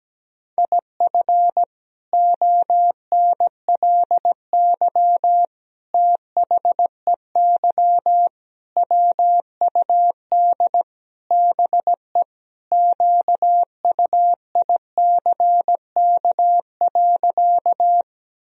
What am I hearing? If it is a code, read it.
Morse code, heard as a keyed tone, about 17 wpm, 700 Hz. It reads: IF ONLY THEY WUD BE QUICK.
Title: Sentences from Top 500 Words and Abbreviations 17wpm